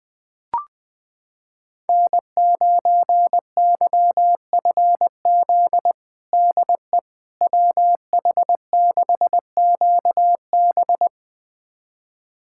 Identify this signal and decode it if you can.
Morse code, heard as a keyed tone, about 21 wpm, 700 Hz. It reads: N9YFZ DE WH6QB
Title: USA Callsign DE Callsign - 5 Characters - 20wpm